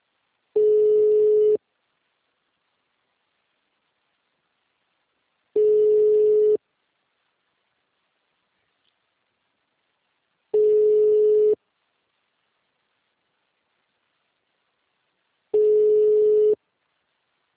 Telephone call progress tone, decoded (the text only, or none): ringback tone